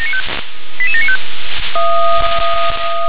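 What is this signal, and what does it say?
Five-tone selective call (ZVEI), heard as quick runs of five tones, tones of about 70 ms. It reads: 88795
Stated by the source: ZVEI five-tone selective call received on an SDR.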